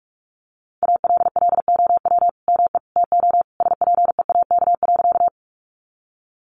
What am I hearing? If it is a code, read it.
Morse code, heard as a keyed tone, about 44 wpm, 700 Hz. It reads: ALLOW ME TO SPEAK.
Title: Sentences from Top 1100 Words 45wpm